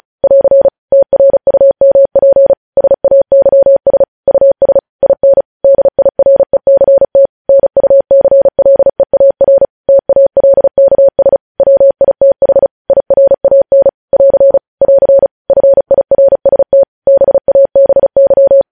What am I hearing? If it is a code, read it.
Morse code, heard as a keyed tone, about 36 wpm, 550 Hz. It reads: <AR> TRUMP SAYS US IN DIRECT NUCLEAR TALKS WITH IRAN <AR> <AR> FIRST BABY
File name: podcast / News headlines-1x-35x35